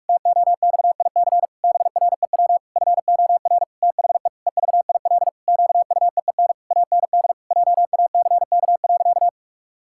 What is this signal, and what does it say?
Morse code, heard as a keyed tone, about 46 wpm, 700 Hz. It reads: TOXIC BREW FOR THE EVIL QUEEN AND JACK.
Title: Pangram Challenge v2 - 45wpm